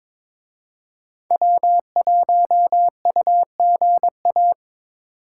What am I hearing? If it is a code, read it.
Morse code, heard as a keyed tone, about 27 wpm, 700 Hz. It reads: W1UGA
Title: Speed Racing - USA Call Signs 22wpm